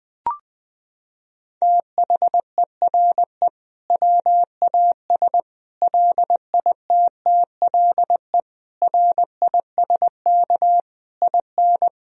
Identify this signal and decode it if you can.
Morse code, heard as a keyed tone, about 21 wpm, 700 Hz. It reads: THERE WAS LITTLE RISK IN
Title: Sentences from Top 1000 Words - Review Part 4 - 20wpm